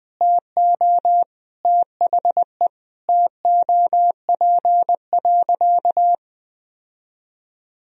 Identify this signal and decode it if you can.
Morse code, heard as a keyed tone, about 20 wpm, 700 Hz. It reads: TO THE TOP.
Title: Sentences from Top 1000 Words - Review Part 3 - 20wpm